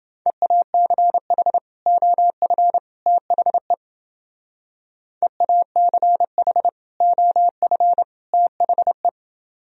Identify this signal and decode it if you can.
Morse code, heard as a keyed tone, about 29 wpm, 700 Hz. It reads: EACH OF THE EACH OF THE